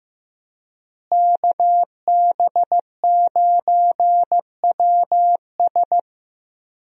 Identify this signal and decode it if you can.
Morse code, heard as a keyed tone, about 15 wpm, 700 Hz. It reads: KB9WS
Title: USA Call Signs - 5 Characters - Repeated 2x - 15wpm